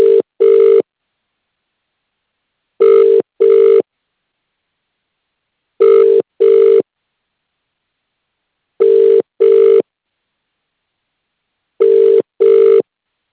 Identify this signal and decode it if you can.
Ringback tone (UK), 400 + 450 Hz.